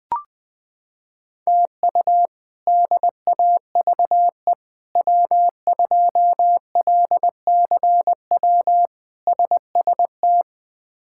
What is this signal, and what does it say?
Morse code, heard as a keyed tone, about 20 wpm, 700 Hz. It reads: TU DAVE W2LCW SST
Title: K1USN SST QSO Practice - 20wpm